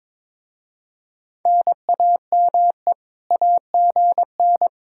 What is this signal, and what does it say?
Morse code, heard as a keyed tone, about 26 wpm, 700 Hz. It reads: NAME AGN
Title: Speed-Racing Common QSO Phrases - 22wpm